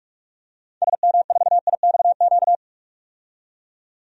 Morse code, heard as a keyed tone, about 46 wpm, 700 Hz. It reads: SM4IXQ
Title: DXCC-100 Call Signs - 6 Characters - 45wpm